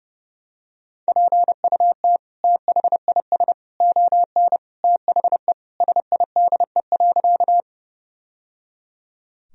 Morse code, heard as a keyed tone, about 29 wpm, 700 Hz. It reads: PUT THIS ON THE SIDE.
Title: Sentences from Top 1000 Words - Review Part 1 - 30wpm